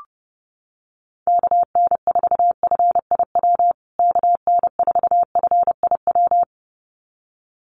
Morse code, heard as a keyed tone, about 30 wpm, 700 Hz. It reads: KN4FIW KN4FIW
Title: USA Call Signs Repeated 2x 30wpm